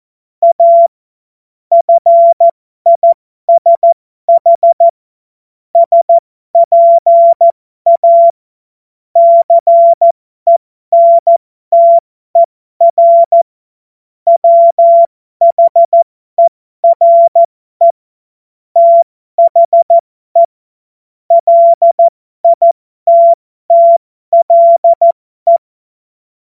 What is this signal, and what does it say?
Morse code, heard as a keyed tone, about 14 wpm, spaced as for 12 wpm, 680 Hz. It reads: A FISH SPA CENTER WHERE THE LITTLE
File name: Morsekurs / Witze